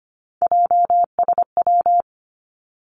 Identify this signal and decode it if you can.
Morse code, heard as a keyed tone, about 25 wpm, 700 Hz. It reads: JSW
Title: Three Challenging Letter-Number Combinations 25wpm